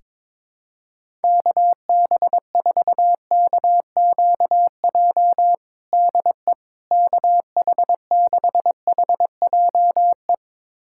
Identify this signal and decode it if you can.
Morse code, heard as a keyed tone, about 22 wpm, 700 Hz. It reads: KB4KQJ DE KH6HJE
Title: USA Callsign DE Callsign 22wpm